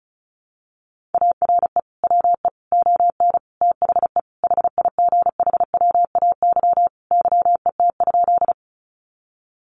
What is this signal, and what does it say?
Morse code, heard as a keyed tone, about 34 wpm, 700 Hz. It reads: ARE WE ON THE HIGHWAY YET?
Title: Sentences from Top 1400 Words 35wpm